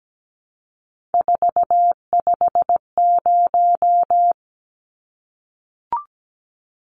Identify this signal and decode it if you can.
Morse code, heard as a keyed tone, about 17 wpm, 700 Hz. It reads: 450
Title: Numbers - 3-Digit 17wpm